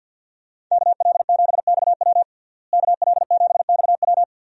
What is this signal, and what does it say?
Morse code, heard as a keyed tone, about 53 wpm, 700 Hz. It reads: KL7XW KL7XW
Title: DXCC-100 Call Signs - 5 Characters - Repeated 2x - 50wpm